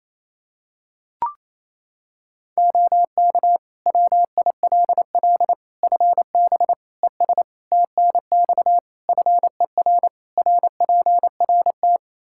Morse code, heard as a keyed tone, about 28 wpm, 700 Hz. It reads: OK WILL FB ES TNX FER RPRT
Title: Practice QSO - Exchange 2 - Info - 28wpm